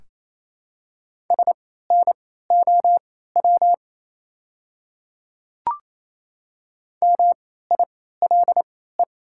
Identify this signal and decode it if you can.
Morse code, heard as a keyed tone, about 27 wpm, spaced as for 16 wpm, 700 Hz. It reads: SNOW MILE
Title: 3x Character Spacing - Top 500 Words - 28wpm